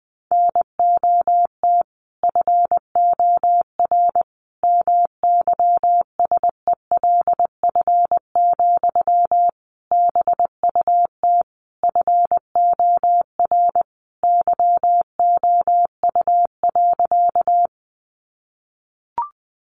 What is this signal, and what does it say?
Morse code, heard as a keyed tone, about 20 wpm, 700 Hz. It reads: NOT FOR MYSELF, BUT FOR YOU.
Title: Sentences from Top 1300 Words 20wpm